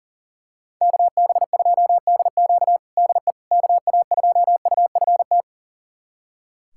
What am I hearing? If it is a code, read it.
Morse code, heard as a keyed tone, about 39 wpm, 700 Hz. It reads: KB2DQ DE KA2UFT
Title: USA Callsign DE Callsign 40wpm